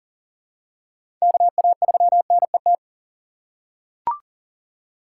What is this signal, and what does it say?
Morse code, heard as a keyed tone, about 40 wpm, 700 Hz. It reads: KA3NET
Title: USA Call Signs - 6 Characters - 3x Character Spacing - 40wpm